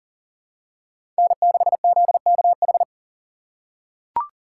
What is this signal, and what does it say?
Morse code, heard as a keyed tone, about 40 wpm, 700 Hz. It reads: N6ZKH
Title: USA Call Signs - 5 Characters - 4x Character Spacing - 40wpm